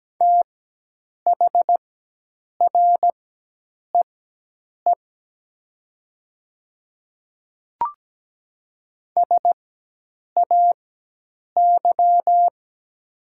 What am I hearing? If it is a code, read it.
Morse code, heard as a keyed tone, about 17 wpm, spaced as for 8 wpm, 700 Hz. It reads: THREE SAY
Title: 4x Character Spacing - Top 300 Words - 17wpm